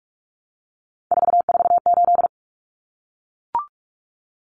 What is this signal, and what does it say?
Morse code, heard as a keyed tone, about 44 wpm, 700 Hz. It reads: 448
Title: Common Cut Numbers - 3-Digit 45wpm